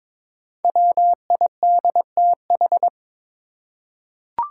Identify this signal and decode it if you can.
Morse code, heard as a keyed tone, about 22 wpm, 700 Hz. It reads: WIDTH
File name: 5 Letter Words - First 3 Letters Given - 22wpm